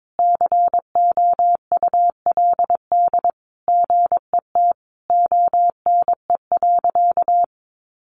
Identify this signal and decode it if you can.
Morse code, heard as a keyed tone, about 22 wpm, 700 Hz. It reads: COULD GET ONE.
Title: Sentences from Top 1400 Words 22wpm